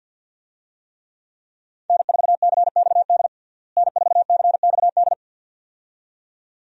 Morse code, heard as a keyed tone, about 52 wpm, 700 Hz. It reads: N4CXD N4CXD
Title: USA Call Signs - 5 Characters - Repeated 2x - 50wpm